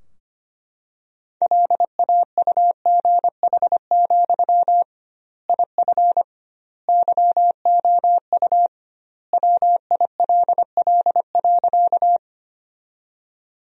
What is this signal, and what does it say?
Morse code, heard as a keyed tone, about 25 wpm, 700 Hz. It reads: LAUGH, IF YOU WILL.
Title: 2x Word-Spacing - Sentences from Top 500 Words - 25wpm